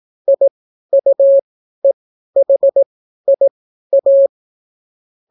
Morse code, heard as a keyed tone, about 18 wpm, spaced as for 12 wpm, 550 Hz. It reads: IUEHIA